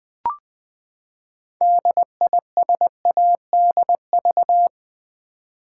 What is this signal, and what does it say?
Morse code, heard as a keyed tone, about 20 wpm, 700 Hz. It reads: DISADV
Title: Build A Word - 6 Letters At A Time - 12-Letter Words - 20wpm